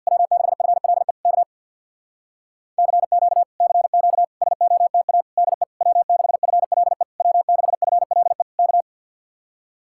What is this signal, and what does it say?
Morse code, heard as a keyed tone, about 56 wpm, 700 Hz. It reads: W6FLE K CQ CQ SOTA DE W6FLE W6FLE K
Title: Speed-Racing Common QSO Phrases - 50wpm